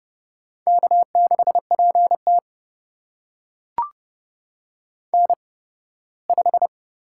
Morse code, heard as a keyed tone, about 31 wpm, spaced as for 18 wpm, 700 Hz. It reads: K6PT N5
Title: USA Call Signs - 8x Character Spacing - 30wpm